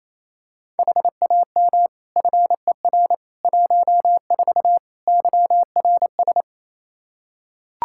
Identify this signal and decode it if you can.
Morse code, heard as a keyed tone, about 28 wpm, 700 Hz. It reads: HAM FER 14 YRS
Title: Practice QSO - Exchange 3 - More Info - 28wpm